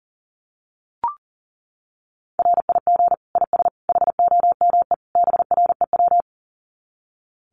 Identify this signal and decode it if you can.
Morse code, heard as a keyed tone, about 40 wpm, 700 Hz. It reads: RIG IS HOME BREW
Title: Common QSO Phrases 40wpm